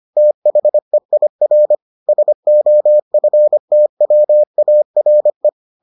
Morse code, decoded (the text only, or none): THEIR SOFTWARE